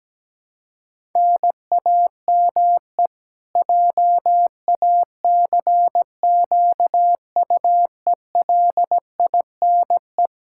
Morse code, heard as a keyed tone, about 17 wpm, 700 Hz. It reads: NAME JACQUELINE
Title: Practice QSO - Exchange 1 - Beginning - 17wpm